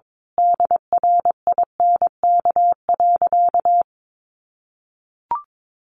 Morse code, heard as a keyed tone, about 22 wpm, 700 Hz. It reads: DRINK.
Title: Sentences from Top 1500 Words 22wpm